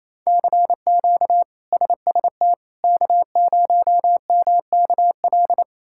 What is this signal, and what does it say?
Morse code, heard as a keyed tone, about 28 wpm, 700 Hz. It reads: CQ SST K0MKL
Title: K1USN SST QSO Practice - 28wpm